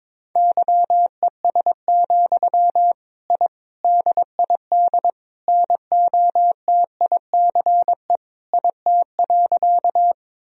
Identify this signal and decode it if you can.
Morse code, heard as a keyed tone, about 22 wpm, 700 Hz. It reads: YES, I DID NOTICE IT.
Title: Sentences from Top 1000 Words - Review Part 2 - 22wpm